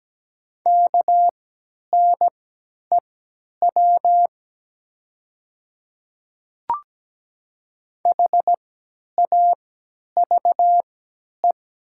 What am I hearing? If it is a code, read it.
Morse code, heard as a keyed tone, about 17 wpm, spaced as for 10 wpm, 700 Hz. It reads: KNEW HAVE
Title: 3x Character Spacing - Top 500 Words - 17wpm